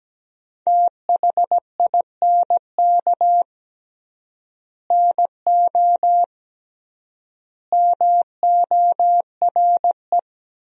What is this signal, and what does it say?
Morse code, heard as a keyed tone, about 17 wpm, 700 Hz. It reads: THINK NO MORE
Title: 3x Word-Spacing - Sentences from Top 200 Words - 17wpm